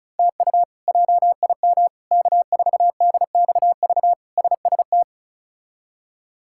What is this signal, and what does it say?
Morse code, heard as a keyed tone, about 36 wpm, 700 Hz. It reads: TU JIM K4DXV SST
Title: K1USN SST QSO Practice - 35wpm